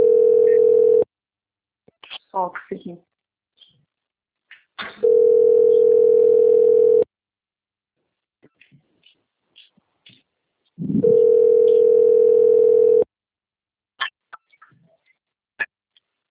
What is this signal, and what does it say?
Ringback tone (North America), 440 + 480 Hz, with speech.
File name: out- - - - - .43731